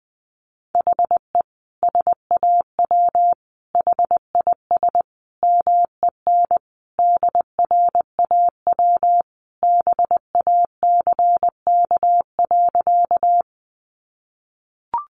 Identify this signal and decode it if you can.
Morse code, heard as a keyed tone, about 20 wpm, 700 Hz. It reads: HE SAW HIS MEN DRAW BACK.
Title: Sentences from Top 300 Words 20wpm